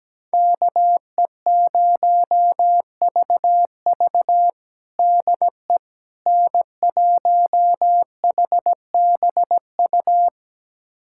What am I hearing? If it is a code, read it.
Morse code, heard as a keyed tone, about 17 wpm, 700 Hz. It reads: KE0VV DE N1HBU
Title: USA Callsign DE Callsign - 5 Characters - 17wpm